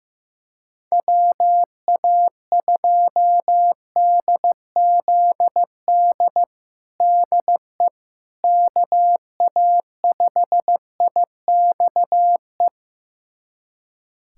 Morse code, heard as a keyed tone, about 16 wpm, 700 Hz. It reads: WA2DZD DE KA5IXE
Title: USA Callsign DE Callsign - 6 Characters - 15wpm